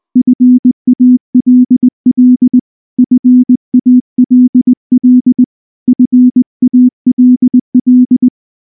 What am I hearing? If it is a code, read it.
Morse code, heard as a keyed tone, about 19 wpm, 250 Hz. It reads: FALL FALL FALL
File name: morse / morse-training-20WPM-9